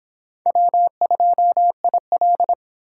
Morse code, heard as a keyed tone, about 25 wpm, 700 Hz. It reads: W2IL